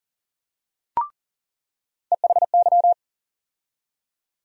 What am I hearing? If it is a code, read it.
Morse code, heard as a keyed tone, about 40 wpm, 700 Hz. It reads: EHY
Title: USA Call Sign Postfixes - 3 Letters 40wpm